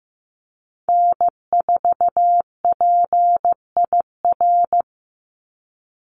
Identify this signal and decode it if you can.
Morse code, heard as a keyed tone, about 15 wpm, 700 Hz. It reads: N4PIR
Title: Call Signs Repeated 3x - 15wpm